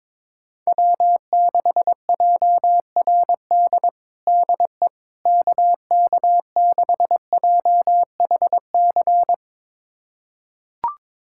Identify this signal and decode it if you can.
Morse code, heard as a keyed tone, about 21 wpm, 700 Hz. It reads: W6JRD DE KK6JHC
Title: USA Callsign DE Callsign 22wpm